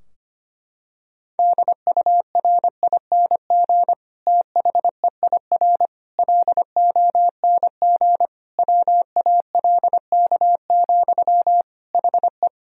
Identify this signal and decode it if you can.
Morse code, heard as a keyed tone, about 25 wpm, 700 Hz. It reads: DURING THEIR LONG WALK, HE
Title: Sentences from Top 1000 Words - Review Part 2 - 25wpm